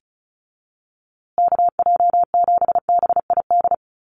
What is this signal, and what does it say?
Morse code, heard as a keyed tone, about 34 wpm, 700 Hz. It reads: KJ7BID
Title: USA Call Signs Repeated 3x 35wpm